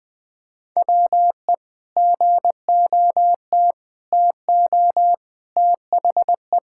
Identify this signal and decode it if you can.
Morse code, heard as a keyed tone, about 21 wpm, 700 Hz. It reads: WE GOT TO THE
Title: Sentences with 5-Word Phrases - 20wpm